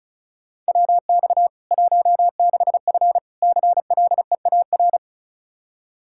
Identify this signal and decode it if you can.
Morse code, heard as a keyed tone, about 36 wpm, 700 Hz. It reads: WX 16F CLEAR
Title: K1USN SST QSO Practice 35wpm